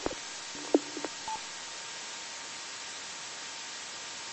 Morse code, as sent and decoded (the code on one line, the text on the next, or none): none